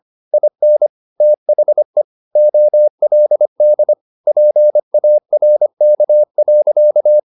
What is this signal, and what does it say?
Morse code, heard as a keyed tone, about 25 wpm, 600 Hz. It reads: IN THE OLD PARK.
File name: the old man and the sea part 1 25wpm